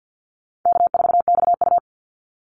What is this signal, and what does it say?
Morse code, heard as a keyed tone, about 50 wpm, 700 Hz. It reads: K4XU
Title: Easy Call Signs - 50wpm